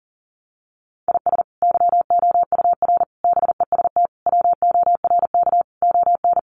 Morse code, heard as a keyed tone, about 40 wpm, 700 Hz. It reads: IS YOUR BEST WORK ON
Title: Sentences from Top 1200 Words 40wpm